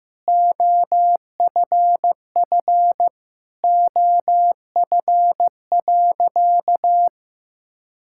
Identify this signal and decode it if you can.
Morse code, heard as a keyed tone, about 15 wpm, 700 Hz. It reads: OFF OF.
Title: Sentences from Top 1000 Words - Review Part 5 - 15wpm